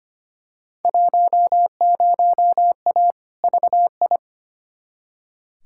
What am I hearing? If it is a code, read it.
Morse code, heard as a keyed tone, about 25 wpm, 700 Hz. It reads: 10A VI
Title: Field Day Exchanges Encore - 25wpm